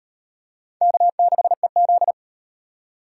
Morse code, heard as a keyed tone, about 35 wpm, 700 Hz. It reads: K6EZ